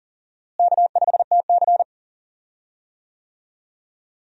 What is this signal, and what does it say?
Morse code, heard as a keyed tone, about 40 wpm, 700 Hz. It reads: K5TC
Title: USA Call Signs - 4 Characters - Repeated 2x - 40wpm